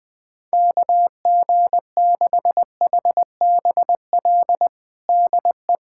Morse code, heard as a keyed tone, about 20 wpm, 700 Hz. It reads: KG6HBL DE
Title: K1USN SST QSO Practice 20wpm